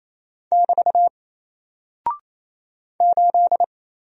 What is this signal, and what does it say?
Morse code, heard as a keyed tone, about 27 wpm, 700 Hz. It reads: <BT> 8
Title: Single Character (Letter:Number:Punctuation:Prosign) 28wpm